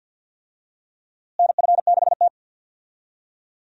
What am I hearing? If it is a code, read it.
Morse code, heard as a keyed tone, about 52 wpm, 700 Hz. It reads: NF6T
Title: USA Call Signs - 4 Characters - Repeated 2x - 50wpm